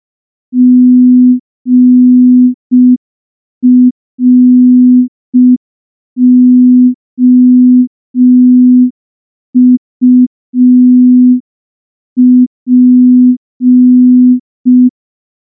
Morse code, heard as a keyed tone, about 5 wpm, 250 Hz. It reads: GROUP
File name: morse / morse-training-05WPM